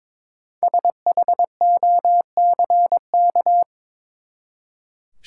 Morse code, heard as a keyed tone, about 22 wpm, 700 Hz. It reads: SHOCK
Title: Five Letter Words - 22wpm